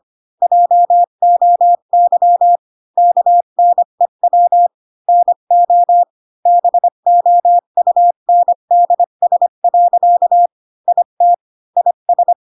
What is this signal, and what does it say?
Morse code, heard as a keyed tone, about 25 wpm, 700 Hz. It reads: JOY KNEW NO BOUNDS. IT IS
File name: Pinocchio / Pinocchio ch5 20 25 ik1hge